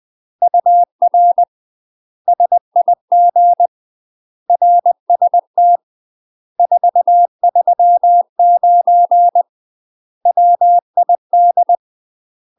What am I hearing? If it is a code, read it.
Morse code, heard as a keyed tone, about 20 wpm, 700 Hz. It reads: UR SIG RST 439 WID
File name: UK MASTER QSOs TRAINING / 20WPM / UK MASTER QSOs 0069